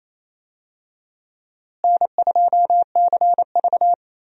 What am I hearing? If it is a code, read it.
Morse code, heard as a keyed tone, about 28 wpm, 700 Hz. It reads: N2CV
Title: USA Call Signs - 4 Characters - Repeated 2x - 28wpm